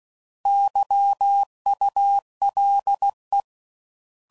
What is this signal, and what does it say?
Morse code, heard as a keyed tone, about 16 wpm, 800 Hz. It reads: YULE
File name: Prefixes / By First Letter / Xs